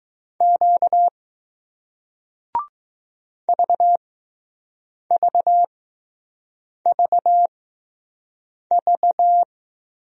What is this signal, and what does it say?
Morse code, heard as a keyed tone, about 20 wpm, spaced as for 6 wpm, 700 Hz. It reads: Q VVVV